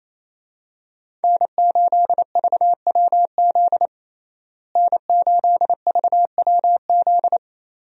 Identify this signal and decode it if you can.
Morse code, heard as a keyed tone, about 27 wpm, 700 Hz. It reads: N8VWZ N8VWZ
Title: USA Call Signs - 5 Characters - Repeated 3x - 28wpm